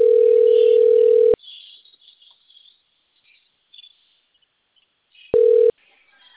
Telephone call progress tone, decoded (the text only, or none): ringback tone